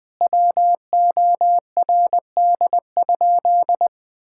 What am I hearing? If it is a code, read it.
Morse code, heard as a keyed tone, about 20 wpm, 700 Hz. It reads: WORD?
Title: Sentences from Top 1000 Words - Review Part 6 - 20wpm